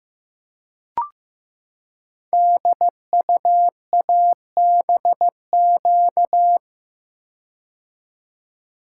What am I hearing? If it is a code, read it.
Morse code, heard as a keyed tone, about 15 wpm, 700 Hz. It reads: DUABQ
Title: Code Groups - 5 Letters - 15wpm